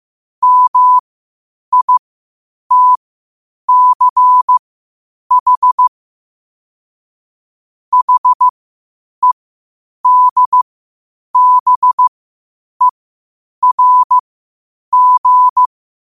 Morse code, heard as a keyed tone, about 15 wpm, spaced as for 8 wpm, 1000 Hz. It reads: MITCH HEDBERG
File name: qotdfiles / 8wpm